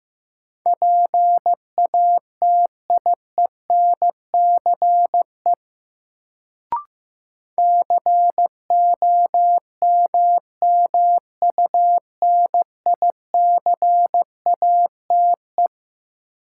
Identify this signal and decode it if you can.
Morse code, heard as a keyed tone, about 15 wpm, 700 Hz. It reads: PATIENCE COMMUNICATE
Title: Sentences from Top 1300 Words - New Word Drill - 15wpm